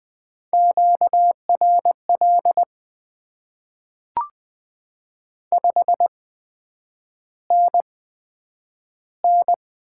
Morse code, heard as a keyed tone, about 20 wpm, spaced as for 9 wpm, 700 Hz. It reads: QRL 5NN
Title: Top 20 QSO Elements - 8x Character Spacing - 20wpm